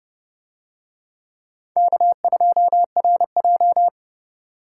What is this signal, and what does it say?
Morse code, heard as a keyed tone, about 30 wpm, 700 Hz. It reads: K2RJ